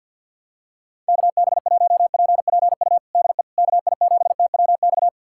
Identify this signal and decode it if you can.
Morse code, heard as a keyed tone, about 51 wpm, 700 Hz. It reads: KB1PPU DE KI7TWX
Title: USA Callsign DE Callsign - 6 Characters - 50wpm